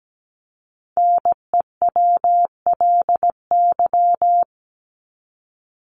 Morse code, heard as a keyed tone, about 17 wpm, 700 Hz. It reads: NEWLY
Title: LY Suffix - Top 100 Words with Base Words 17wpm